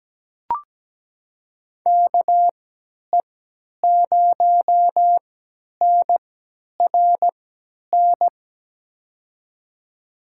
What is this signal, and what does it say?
Morse code, heard as a keyed tone, about 17 wpm, spaced as for 10 wpm, 700 Hz. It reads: KE0NRN
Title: USA Call Signs - 6 Characters - 3x Character Spacing - 17wpm